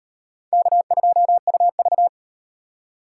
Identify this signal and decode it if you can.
Morse code, heard as a keyed tone, about 37 wpm, 700 Hz. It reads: K2UV